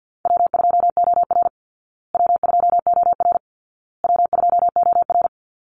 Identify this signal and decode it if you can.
Morse code, heard as a keyed tone, about 50 wpm, 700 Hz. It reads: W2OR W2OR W2OR
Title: Easy Call Signs Repeated 3x - 50wpm